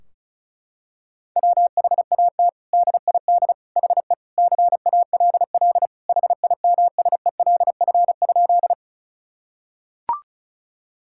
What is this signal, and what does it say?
Morse code, heard as a keyed tone, about 36 wpm, 700 Hz. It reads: WHAT DID HE CALL HIMSELF?
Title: Sentences from Top 1600 Words 35wpm